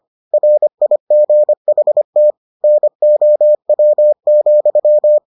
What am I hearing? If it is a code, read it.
Morse code, heard as a keyed tone, about 25 wpm, 600 Hz. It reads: RIGHT NOW,